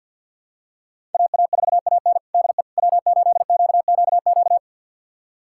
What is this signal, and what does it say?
Morse code, heard as a keyed tone, about 52 wpm, 700 Hz. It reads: AA4AN DE W8QQQ
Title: USA Callsign DE Callsign 50wpm